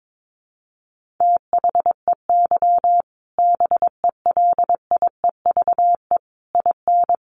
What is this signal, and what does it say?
Morse code, heard as a keyed tone, about 22 wpm, 700 Hz. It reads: THEY BELIEVE IN
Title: Sentences from Top 1000 Words 22wpm